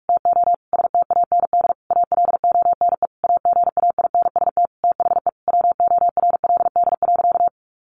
Morse code, heard as a keyed tone, about 44 wpm, 700 Hz. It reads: TO STAND ALONE AGAINST THE WORLD.
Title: Sentences from Top 1400 Words 45wpm